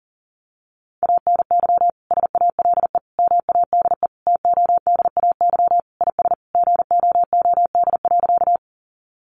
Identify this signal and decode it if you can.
Morse code, heard as a keyed tone, about 40 wpm, 700 Hz. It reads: ANY SALE MADE TODAY IS GOOD.
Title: Sentences from Top 1400 Words 40wpm